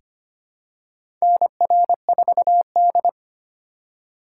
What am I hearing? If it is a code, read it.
Morse code, heard as a keyed tone, about 25 wpm, 700 Hz. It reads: NR4D
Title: USA Call Signs - 2x Character Spacing - 25wpm